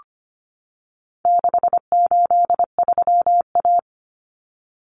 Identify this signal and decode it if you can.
Morse code, heard as a keyed tone, about 25 wpm, 700 Hz. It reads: 683A
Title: Common Cut Numbers - 4-Digit 25wpm